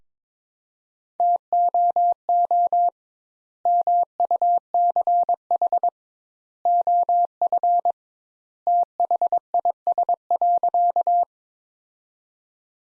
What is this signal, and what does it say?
Morse code, heard as a keyed tone, about 22 wpm, 700 Hz. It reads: TOO MUCH OF THIS.
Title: Sentences Top 200 Words - 22wpm